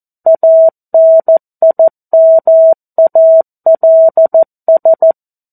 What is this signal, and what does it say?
Morse code, heard as a keyed tone, about 14 wpm, 650 Hz. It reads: ANIMALS
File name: 02-Chapter-02 14 WPM